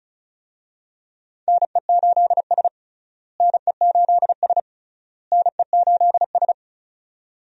Morse code, heard as a keyed tone, about 36 wpm, 700 Hz. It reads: NE8S NE8S NE8S
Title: USA Call Signs - 4 Characters - Repeated 3x - 35wpm